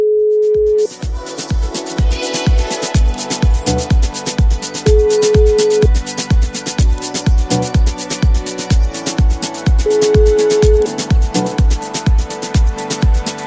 Ringback tone (Europe, 425 Hz), with noise.